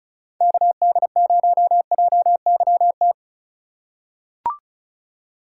Morse code, heard as a keyed tone, about 34 wpm, 700 Hz. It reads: KD0JYT
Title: USA Call Signs - 6 Characters - 2x Character Spacing - 35wpm